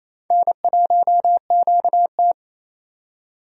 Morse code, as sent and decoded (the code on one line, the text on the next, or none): -. .---- --.- -
N1QT